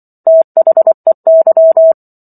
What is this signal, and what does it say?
Morse code, heard as a keyed tone, about 24 wpm, 650 Hz. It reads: THEY